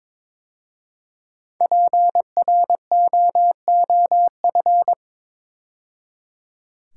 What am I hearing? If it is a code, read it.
Morse code, heard as a keyed tone, about 23 wpm, 690 Hz. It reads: PROOF
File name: 5 Letter Words - First 3 Letters Given - 22wpm